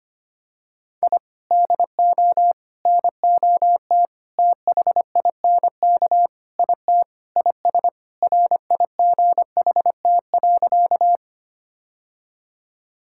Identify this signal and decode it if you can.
Morse code, heard as a keyed tone, about 25 wpm, 700 Hz. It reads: I DO NOT THINK IT IS RIGHT.
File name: practice / 25wpm / Sentences with 4-Word Phrases - 25wpm